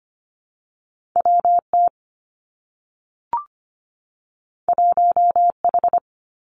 Morse code, heard as a keyed tone, about 25 wpm, 700 Hz. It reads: WT 1H
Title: Two Challenging Letter-Number Combinations 25wpm